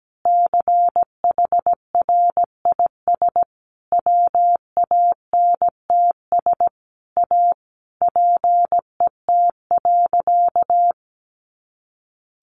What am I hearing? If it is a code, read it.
Morse code, heard as a keyed tone, about 17 wpm, 700 Hz. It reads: CHRIS WANTS A PET.
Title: Easy Phrases 17wpm